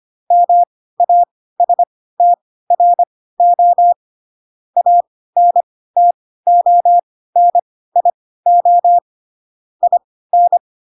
Morse code, heard as a keyed tone, about 25 wpm, spaced as for 16 wpm, 700 Hz. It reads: MASTRO ANTONIO IN
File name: Pinocchio / Pinocchio ch2 10 25 ik1hge